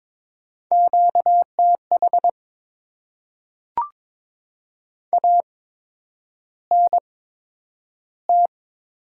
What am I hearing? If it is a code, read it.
Morse code, heard as a keyed tone, about 22 wpm, spaced as for 9 wpm, 700 Hz. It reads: QTH ANT
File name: Top 20 QSO Elements - 8x Character Spacing - 22wpm